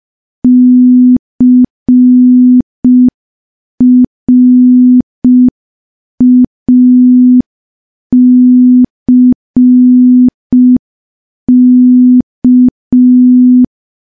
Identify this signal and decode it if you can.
Morse code, heard as a keyed tone, about 5 wpm, 250 Hz. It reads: CRACK
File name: morse / morse-training-05WPM